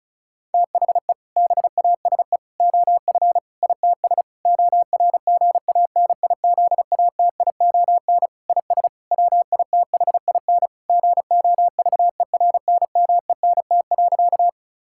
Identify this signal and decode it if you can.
Morse code, heard as a keyed tone, about 34 wpm, 700 Hz. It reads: THE BASE OF ITS ORGANIZATION IS WITHIN GOVERNMENT.
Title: Sentences from Top 1000 Words - Review Part 5 - 35wpm